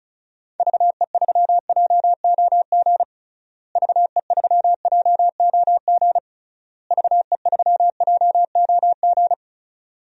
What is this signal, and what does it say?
Morse code, heard as a keyed tone, about 34 wpm, 700 Hz. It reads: VE3JOG VE3JOG VE3JOG
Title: DXCC-100 Call Signs - 6 Characters - Repeated 3x - 35wpm